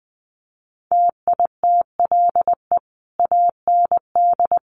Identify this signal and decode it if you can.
Morse code, heard as a keyed tone, about 20 wpm, 700 Hz. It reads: TITLE AND
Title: Sentences from Top 1200 Words 20wpm